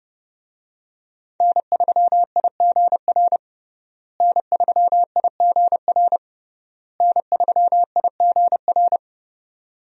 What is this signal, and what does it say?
Morse code, heard as a keyed tone, about 30 wpm, 700 Hz. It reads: N3IGR N3IGR N3IGR
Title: USA Call Signs - 5 Characters - Repeated 3x - 30wpm